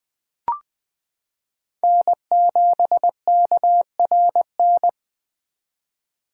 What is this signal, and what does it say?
Morse code, heard as a keyed tone, about 21 wpm, 700 Hz. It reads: N7KRN
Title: 2023 SOTA Chaser Call Signs - 20wpm